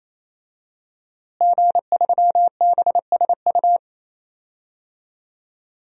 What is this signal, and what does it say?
Morse code, heard as a keyed tone, about 28 wpm, 700 Hz. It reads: G3BSU
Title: DXCC-100 Call Signs - 5 Characters - Repeated 3x - 28wpm